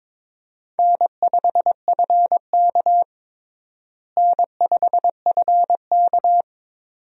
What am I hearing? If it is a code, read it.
Morse code, heard as a keyed tone, about 22 wpm, 700 Hz. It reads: N5FK N5FK
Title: USA Call Signs - 4 Characters - Repeated 2x - 22wpm